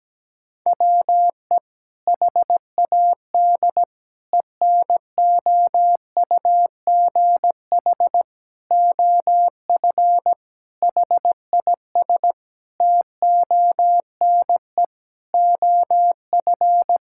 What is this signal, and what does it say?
Morse code, heard as a keyed tone, about 17 wpm, 700 Hz. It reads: WE HAD ENOUGH OF HIS TONE OF
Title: Sentences from Top 1600 Words 17wpm